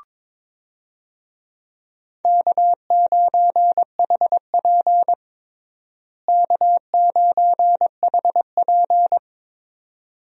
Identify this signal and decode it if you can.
Morse code, heard as a keyed tone, about 22 wpm, 700 Hz. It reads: K9HP K9HP
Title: USA Call Signs - 4 Characters - Repeated 2x - 22wpm